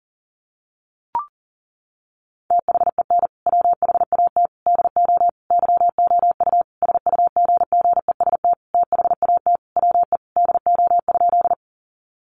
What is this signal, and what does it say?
Morse code, heard as a keyed tone, about 40 wpm, 700 Hz. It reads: THEN WHAT DO YOU SUGGEST THAT WE DO?
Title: Sentences from Top 1300 Words 40wpm